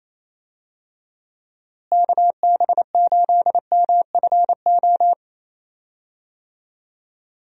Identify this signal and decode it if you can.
Morse code, heard as a keyed tone, about 28 wpm, 700 Hz. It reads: KB8MFO
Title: USA Call Signs - 6 Characters - Repeated 2x - 28wpm